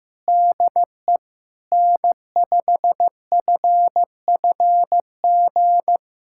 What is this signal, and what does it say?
Morse code, heard as a keyed tone, about 16 wpm, 700 Hz. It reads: DE N5FFG
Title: USA Callsign DE Callsign - 5 Characters - 15wpm